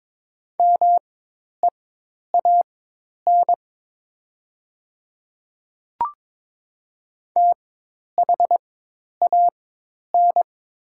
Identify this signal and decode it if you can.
Morse code, heard as a keyed tone, about 22 wpm, spaced as for 10 wpm, 700 Hz. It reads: MEAN THAN
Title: 4x Character Spacing - Top 300 Words - 22wpm